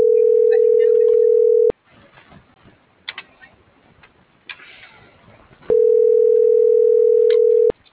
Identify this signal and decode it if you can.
Ringback tone (North America), 440 + 480 Hz.